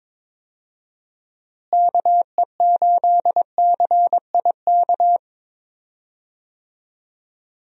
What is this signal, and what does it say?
Morse code, heard as a keyed tone, about 22 wpm, 700 Hz. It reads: KE8CIK
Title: USA Call Signs - 6 Characters - Repeated 3x - 22wpm